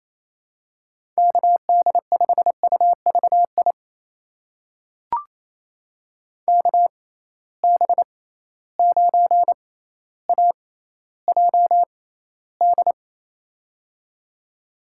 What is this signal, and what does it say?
Morse code, heard as a keyed tone, about 28 wpm, spaced as for 14 wpm, 700 Hz. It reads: KD5UVI KB9AJD
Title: USA Call Signs - 6 Characters - 6x Character Spacing - 28wpm